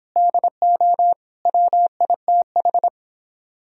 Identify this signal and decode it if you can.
Morse code, heard as a keyed tone, about 29 wpm, 700 Hz. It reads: DO WITH